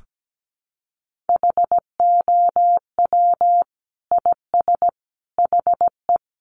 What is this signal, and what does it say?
Morse code, heard as a keyed tone, about 17 wpm, 700 Hz. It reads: HOW IS HE
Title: Sentences from Top 100 Words 17wpm